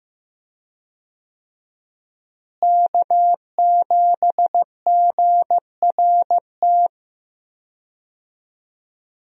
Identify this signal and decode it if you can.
Morse code, heard as a keyed tone, about 15 wpm, 700 Hz. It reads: K7GRT
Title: USA Call Signs - 5 Characters - Repeated 3x - 15wpm